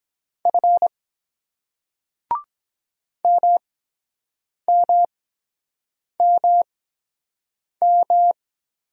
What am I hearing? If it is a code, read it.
Morse code, heard as a keyed tone, about 23 wpm, spaced as for 7 wpm, 700 Hz. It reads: F MMMM